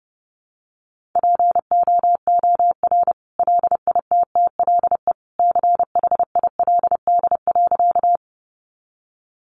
Morse code, heard as a keyed tone, about 30 wpm, 700 Hz. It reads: POOR LITTLE CHILD.
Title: Sentences from Top 1000 Words 30wpm